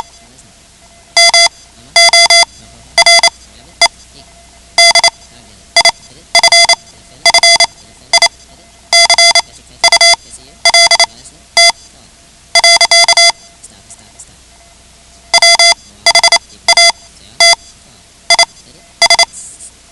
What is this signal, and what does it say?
Morse code, heard as a keyed tone, about 28 wpm, spaced as for 13 wpm, 2050 Hz. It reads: MORE DIFFICULT. WHAT IS
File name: ham radio / CodeQuick / Code Quick 10